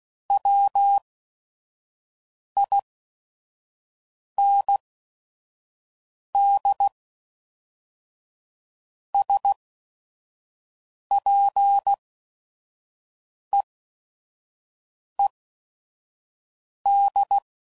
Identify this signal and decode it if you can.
Morse code, heard as a keyed tone, about 16 wpm, spaced as for 5 wpm, 800 Hz. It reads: WIND SPEED
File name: CW Code Groups @5WPM / Weather Sets / Weather 4 28